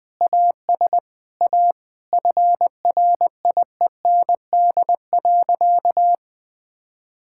Morse code, heard as a keyed tone, about 21 wpm, 700 Hz. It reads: AS A FRIEND.
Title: Sentences from Top 1000 Words - Review Part 6 - 20wpm